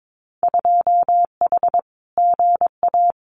Morse code, heard as a keyed tone, about 22 wpm, 700 Hz. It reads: 2H GA
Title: Winter Field Day Exchanges Encore 22wpm